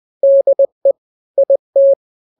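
Morse code, heard as a keyed tone, about 20 wpm, 550 Hz. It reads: DE IT
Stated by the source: Morse drill of two-letter n-grams at 20/18 WPM (Farnsworth), from KK7RR.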